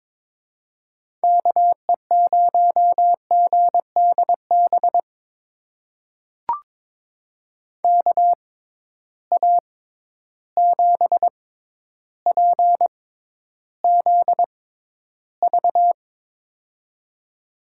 Morse code, heard as a keyed tone, about 22 wpm, spaced as for 11 wpm, 700 Hz. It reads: KE0GDB KA7PZV
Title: USA Call Signs - 6 Characters - 6x Character Spacing - 22wpm